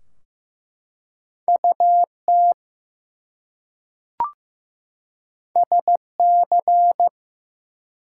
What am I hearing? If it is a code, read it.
Morse code, heard as a keyed tone, about 15 wpm, 700 Hz. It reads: UT SC
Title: US States and Canadian Provinces 15wpm